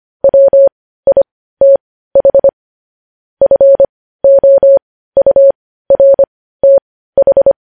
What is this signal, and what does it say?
Morse code, heard as a keyed tone, about 25 wpm, spaced as for 15 wpm, 550 Hz. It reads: WITH FOURTH
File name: podcast / News headlines-1x-25x15